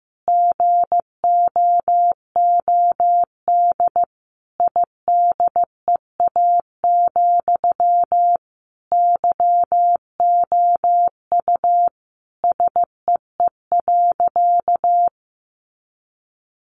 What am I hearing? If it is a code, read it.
Morse code, heard as a keyed tone, about 15 wpm, 700 Hz. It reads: GOOD IDEA, YOU SEE.
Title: Sentences from Top 600 Words 15wpm